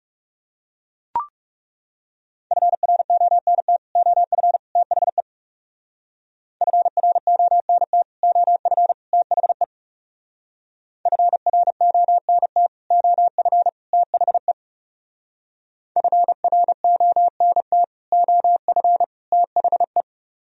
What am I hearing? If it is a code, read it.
Morse code, heard as a keyed tone, about 36 wpm, 700 Hz. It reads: FRONT OF THE FRONT OF THE FRONT OF THE FRONT OF THE
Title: Speed-Racing - Sets of 3 Words 30wpm